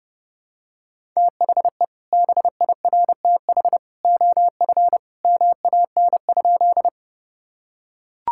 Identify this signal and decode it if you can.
Morse code, heard as a keyed tone, about 30 wpm, 700 Hz. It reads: THE BIRTH OF MAN?
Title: Sentences from Top 1000 Words - Review Part 5 - 30wpm